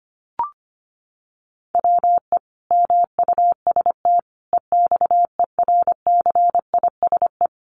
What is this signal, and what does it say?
Morse code, heard as a keyed tone, about 25 wpm, 700 Hz. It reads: WE MUST EXERCISE
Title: Sentences from Top 1200 Words 25wpm